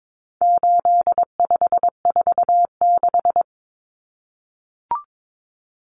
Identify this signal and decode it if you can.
Morse code, heard as a keyed tone, about 22 wpm, 700 Hz. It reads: 8546
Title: Common Cut Numbers - 4-Digit 22wpm